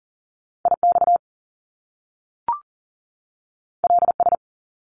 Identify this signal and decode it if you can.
Morse code, heard as a keyed tone, about 39 wpm, 700 Hz. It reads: IX LS
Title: Top Two Letter Combinations in Words 40wpm